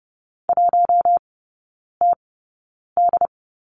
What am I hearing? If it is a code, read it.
Morse code, heard as a keyed tone, about 30 wpm, spaced as for 9 wpm, 700 Hz. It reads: 1TD
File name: 081 Lesson 14e - Number 2 - Three Characters 30wpm